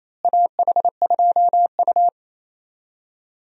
Morse code, heard as a keyed tone, about 28 wpm, 700 Hz. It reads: AH2U
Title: USA Call Signs - 4 Characters - Repeated 3x - 28wpm